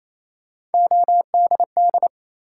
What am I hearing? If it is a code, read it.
Morse code, heard as a keyed tone, about 28 wpm, 700 Hz. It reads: ODD
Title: Three Letter Words - 28wpm